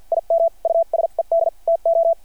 Morse code, heard as a keyed tone, about 50 wpm, spaced as for 42 wpm, 670 Hz. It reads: IM USED TO
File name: archived / CRSnet